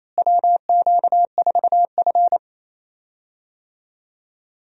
Morse code, heard as a keyed tone, about 28 wpm, 700 Hz. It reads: WQ4F
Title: USA Call Signs - 4 Characters - Repeated 2x - 28wpm